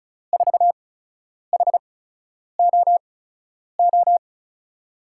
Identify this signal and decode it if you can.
Morse code, heard as a keyed tone, about 36 wpm, spaced as for 10 wpm, 700 Hz. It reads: 4HOO
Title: USA Call Signs - 6 Characters - 8x Character Spacing - 35wpm